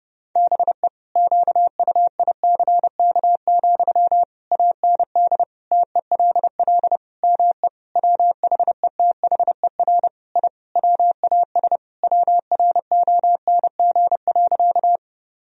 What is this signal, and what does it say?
Morse code, heard as a keyed tone, about 30 wpm, 700 Hz. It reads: BE QUICK, AND TELL ME WHETHER I WAS WRONG.
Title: Sentences from Top 1000 Words - Review Part 3 - 30wpm